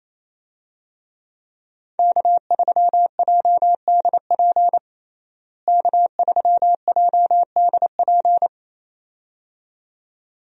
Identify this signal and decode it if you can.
Morse code, heard as a keyed tone, about 28 wpm, 700 Hz. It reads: K3JDP K3JDP
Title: USA Call Signs - 5 Characters - Repeated 2x - 28wpm